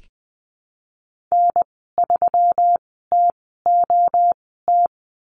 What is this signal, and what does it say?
Morse code, heard as a keyed tone, about 20 wpm, spaced as for 15 wpm, 700 Hz. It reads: N3TOT
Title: 147 Lesson 24g - Letter y - Call Signs 20wpm